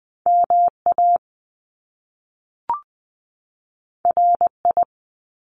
Morse code, heard as a keyed tone, about 20 wpm, 700 Hz. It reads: MA RI
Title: US States 20wpm